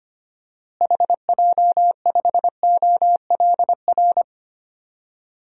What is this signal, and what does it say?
Morse code, heard as a keyed tone, about 25 wpm, 700 Hz. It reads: HJ5OLR
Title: DXCC-100 Call Signs - 6 Characters - Repeated 3x - 25wpm